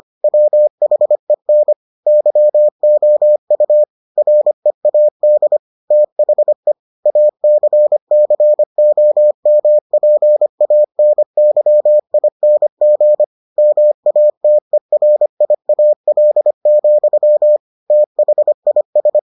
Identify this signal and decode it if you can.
Morse code, heard as a keyed tone, about 25 wpm, 600 Hz. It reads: WHEN YOU READ THE ACCOMPANYING MATERIAL, THIS